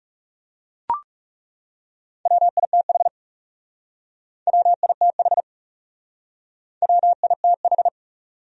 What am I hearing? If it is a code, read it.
Morse code, heard as a keyed tone, about 39 wpm, 700 Hz. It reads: WITH WITH WITH